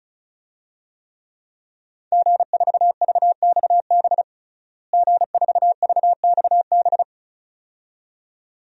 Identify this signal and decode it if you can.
Morse code, heard as a keyed tone, about 36 wpm, 700 Hz. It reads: G4VXB G4VXB
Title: DXCC-100 Call Signs - 5 Characters - Repeated 2x - 35wpm